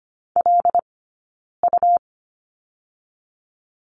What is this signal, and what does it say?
Morse code, heard as a keyed tone, about 25 wpm, 700 Hz. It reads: L U
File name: Lesson 17d - Letter u - Two Characters 25wpm